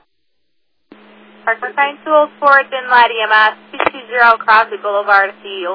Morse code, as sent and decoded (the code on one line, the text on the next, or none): none